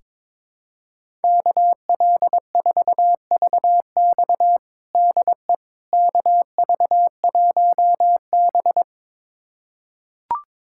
Morse code, heard as a keyed tone, about 22 wpm, 700 Hz. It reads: KL4VX DE KV1B
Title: USA Callsign DE Callsign 22wpm